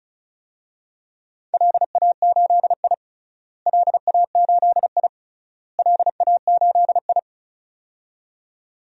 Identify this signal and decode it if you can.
Morse code, heard as a keyed tone, about 36 wpm, 700 Hz. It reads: LA8I LA8I LA8I
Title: DXCC-100 Call Signs - 4 Characters - Repeated 3x - 35wpm